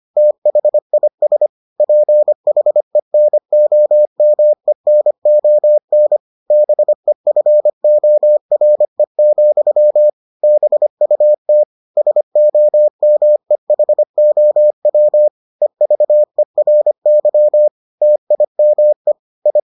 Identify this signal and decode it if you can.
Morse code, heard as a keyed tone, about 25 wpm, 600 Hz. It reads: THIS PHENOMENON BEFORE, BUT SOMEHOW EVERY TIME I